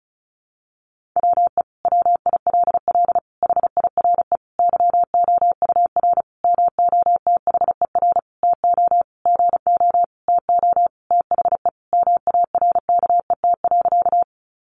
Morse code, heard as a keyed tone, about 34 wpm, 700 Hz. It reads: WE WILL HIRE YOUR MOTHER TO GO TO THE MARKET.
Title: Sentences from Top 1400 Words 35wpm